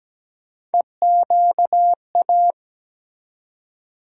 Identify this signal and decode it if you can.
Morse code, heard as a keyed tone, about 17 wpm, 700 Hz. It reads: EQA
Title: USA Call Sign Postfixes - 3 Letters 17wpm